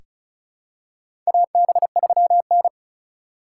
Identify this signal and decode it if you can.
Morse code, heard as a keyed tone, about 34 wpm, 700 Hz. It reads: AB3N
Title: USA Call Signs - 4 Characters - 6x Character Spacing - 35wpm